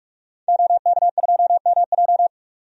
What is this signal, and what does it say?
Morse code, heard as a keyed tone, about 45 wpm, 700 Hz. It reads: OK2MJ
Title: DXCC-100 Call Signs - 5 Characters - 45wpm